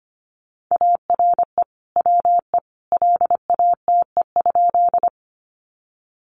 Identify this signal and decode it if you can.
Morse code, heard as a keyed tone, about 25 wpm, 700 Hz. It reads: ARE WE LATE?
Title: Sentences from Top 300 Words 25wpm